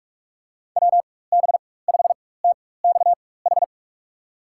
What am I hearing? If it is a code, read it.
Morse code, heard as a keyed tone, about 46 wpm, spaced as for 21 wpm, 700 Hz. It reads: WB5TXH
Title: USA Call Signs - 6 Characters - 4x Character Spacing - 45wpm